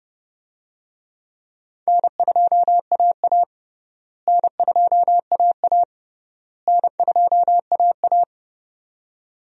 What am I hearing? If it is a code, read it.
Morse code, heard as a keyed tone, about 29 wpm, 700 Hz. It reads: N2AA N2AA N2AA
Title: USA Call Signs - 4 Characters - Repeated 3x - 30wpm